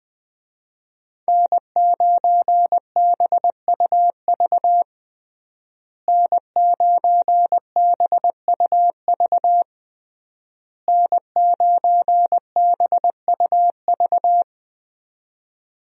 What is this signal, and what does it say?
Morse code, heard as a keyed tone, about 20 wpm, 700 Hz. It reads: N9BUV N9BUV N9BUV
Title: USA Call Signs - 5 Characters - Repeated 3x - 20wpm